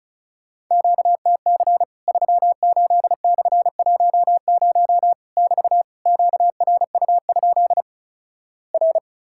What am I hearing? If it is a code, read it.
Morse code, heard as a keyed tone, about 36 wpm, 700 Hz. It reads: QTC 38/10 <BT> QRU? R
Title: Worked All Europe DX Contest - QTC Exchange - 35wpm